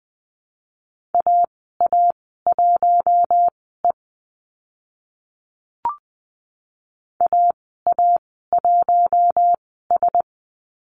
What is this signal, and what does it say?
Morse code, heard as a keyed tone, about 20 wpm, spaced as for 15 wpm, 700 Hz. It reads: AA1E AA1S
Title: 042 Lesson 8g - Number 1 - Call Signs 20wpm